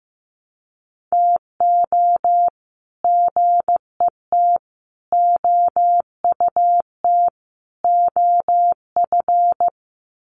Morse code, heard as a keyed tone, about 15 wpm, 700 Hz. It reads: TO GET OUT OF
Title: Sets of 4 Words 15wpm